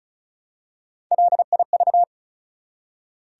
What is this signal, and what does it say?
Morse code, heard as a keyed tone, about 34 wpm, 700 Hz. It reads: LIV
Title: USA Common Names - Max 4 Letters - 35wpm